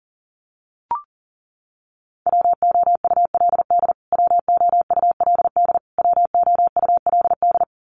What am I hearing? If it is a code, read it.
Morse code, heard as a keyed tone, about 40 wpm, 700 Hz. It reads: WOULD WOULD WOULD
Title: Top 500 Words Repeated 40wpm